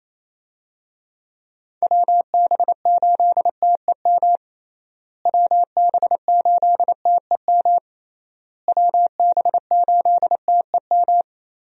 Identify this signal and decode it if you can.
Morse code, heard as a keyed tone, about 28 wpm, 700 Hz. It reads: WB8TEM WB8TEM WB8TEM
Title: USA Call Signs - 6 Characters - Repeated 3x - 28wpm